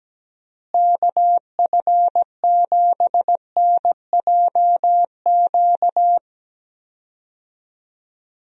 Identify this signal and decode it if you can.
Morse code, heard as a keyed tone, about 17 wpm, 700 Hz. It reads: KF7NJQ
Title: USA Call Signs - 6 Characters - Repeated 3x - 17wpm